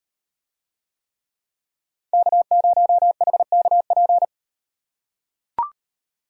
Morse code, heard as a keyed tone, about 38 wpm, 700 Hz. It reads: K0HKP